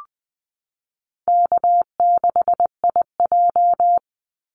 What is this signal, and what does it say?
Morse code, heard as a keyed tone, about 20 wpm, 700 Hz. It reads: K6IJ
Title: Easy Call Signs Repeated 3x - 20wpm